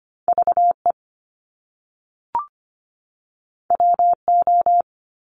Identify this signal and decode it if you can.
Morse code, heard as a keyed tone, about 25 wpm, 700 Hz. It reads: VE WO
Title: Top Two Letter Combinations in Words 25wpm